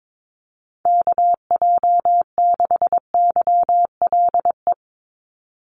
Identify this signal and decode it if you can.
Morse code, heard as a keyed tone, about 22 wpm, 700 Hz. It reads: KJ6YLE
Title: USA Call Signs 22wpm